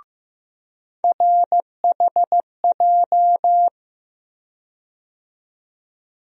Morse code, heard as a keyed tone, about 15 wpm, 700 Hz. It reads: RHJ
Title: USA Call Sign Postfixes - 3 Letters 15wpm